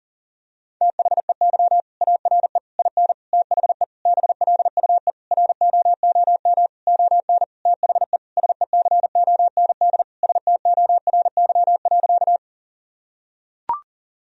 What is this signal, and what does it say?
Morse code, heard as a keyed tone, about 40 wpm, 700 Hz. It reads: THEY ARE IN THE BLUE ROOM ON THE SECOND STORY.
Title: Sentences from Top 1000 Words - Review Part 3 - 40wpm